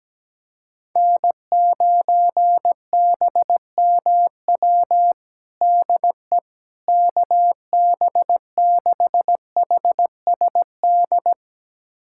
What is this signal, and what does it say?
Morse code, heard as a keyed tone, about 17 wpm, 700 Hz. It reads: N9BMW DE KB6HSD
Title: USA Callsign DE Callsign 17wpm